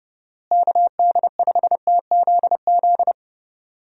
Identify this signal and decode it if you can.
Morse code, heard as a keyed tone, about 32 wpm, 700 Hz. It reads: KD5TZZ
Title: USA Call Signs - 6 Characters - Repeated 2x - 30wpm